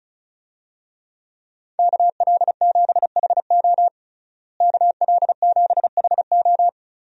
Morse code, heard as a keyed tone, about 34 wpm, 700 Hz. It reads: KL7HO KL7HO
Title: DXCC-100 Call Signs - 5 Characters - Repeated 2x - 35wpm